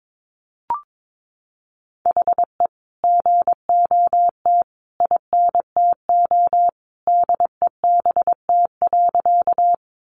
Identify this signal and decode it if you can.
Morse code, heard as a keyed tone, about 22 wpm, 700 Hz. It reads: HE GOT INTO DEBT.
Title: Sentences from Top 1300 Words 22wpm